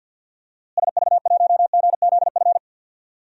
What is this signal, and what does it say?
Morse code, heard as a keyed tone, about 53 wpm, 700 Hz. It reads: SV1GZF
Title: DXCC-100 Call Signs - 6 Characters - Repeated 3x - 50wpm